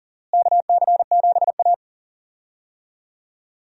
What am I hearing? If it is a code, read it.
Morse code, heard as a keyed tone, about 40 wpm, 700 Hz. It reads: KC7A
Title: USA Call Signs - 4 Characters - Repeated 2x - 40wpm